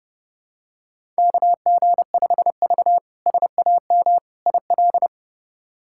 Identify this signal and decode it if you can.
Morse code, heard as a keyed tone, about 30 wpm, 700 Hz. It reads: KG5V SAM IL
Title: K1USN SST QSO Practice - 30wpm